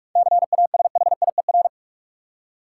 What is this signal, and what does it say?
Morse code, heard as a keyed tone, about 43 wpm, 700 Hz. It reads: CASHIER
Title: Practice QSO - Exchange 3 - More Info - 45wpm